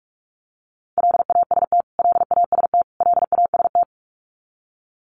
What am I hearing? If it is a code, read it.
Morse code, heard as a keyed tone, about 46 wpm, 700 Hz. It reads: LAST LAST LAST
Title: Top 300 Words Repeated 45wpm